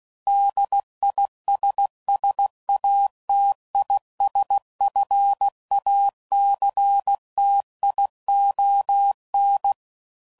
Morse code, heard as a keyed tone, about 16 wpm, 800 Hz. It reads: DISSATISFACTION